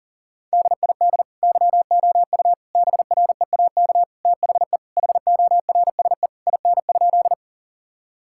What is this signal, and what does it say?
Morse code, heard as a keyed tone, about 37 wpm, 700 Hz. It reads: DID YOU BREAK THE HORSE IN?
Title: Sentences from Top 1000 Words - Review Part 6 - 40wpm